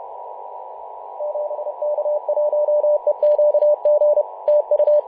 Morse code, heard as a keyed tone, about 31 wpm, 600 Hz, with radio noise. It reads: 7K1EQG TU
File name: Mar/20 2349z FW5JJ 28.011MHz CW